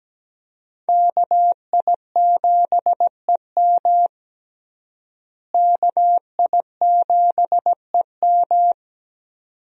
Morse code, heard as a keyed tone, about 17 wpm, 700 Hz. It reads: KI7EM KI7EM
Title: USA Call Signs - 5 Characters - Repeated 2x - 17wpm